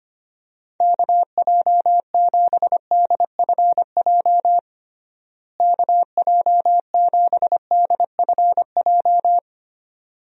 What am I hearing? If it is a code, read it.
Morse code, heard as a keyed tone, about 25 wpm, 700 Hz. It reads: KJ7DFJ KJ7DFJ
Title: USA Call Signs - 6 Characters - Repeated 2x - 25wpm